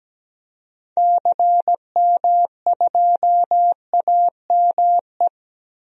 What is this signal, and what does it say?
Morse code, heard as a keyed tone, about 17 wpm, 700 Hz. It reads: CM2AME
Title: DXCC-100 Call Signs - 6 Characters - Repeated 2x - 17wpm